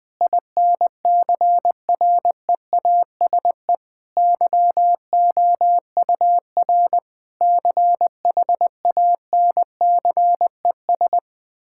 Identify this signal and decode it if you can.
Morse code, heard as a keyed tone, about 21 wpm, 700 Hz. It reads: INCREASE YOUR CHANCES
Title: Sentences with 8-Word Phrases - Spoken Beginning - 20wpm